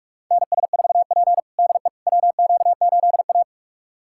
Morse code, heard as a keyed tone, about 45 wpm, 690 Hz. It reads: NS4P DE WQ8A
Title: USA Callsign DE Callsign - 4 Characters - 45wpm